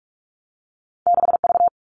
Morse code, heard as a keyed tone, about 46 wpm, 700 Hz. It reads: 6V
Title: All Cut Numbers - 2-Digit 45wpm